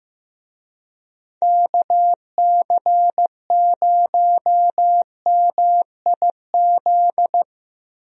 Morse code, heard as a keyed tone, about 15 wpm, 700 Hz. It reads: KC0MIZ
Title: USA Call Signs - 6 Characters - Repeated 2x - 15wpm